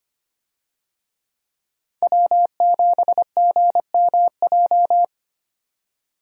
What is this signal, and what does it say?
Morse code, heard as a keyed tone, about 25 wpm, 700 Hz. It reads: W7GMJ
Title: USA Call Signs - 5 Characters - 25wpm